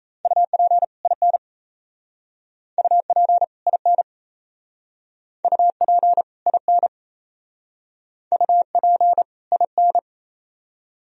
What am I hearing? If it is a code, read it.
Morse code, heard as a keyed tone, about 34 wpm, 700 Hz. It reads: UP IN UP IN UP IN UP IN
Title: Speed-Racing - Sets of 2 Words 28wpm